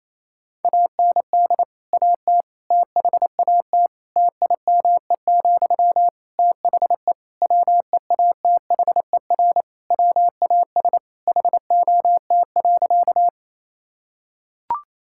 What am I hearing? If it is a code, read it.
Morse code, heard as a keyed tone, about 28 wpm, 700 Hz. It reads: AND AT THAT TIME, THE WEATHER WAS HOT.
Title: Sentences from Top 1000 Words - Review Part 5 - 28wpm